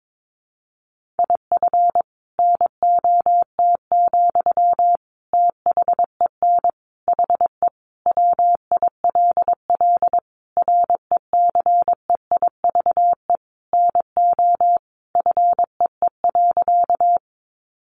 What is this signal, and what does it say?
Morse code, heard as a keyed tone, about 22 wpm, 700 Hz. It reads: IF NOT, THEN HE WILL RECEIVE NO FEE.
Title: Sentences from Top 1500 Words 22wpm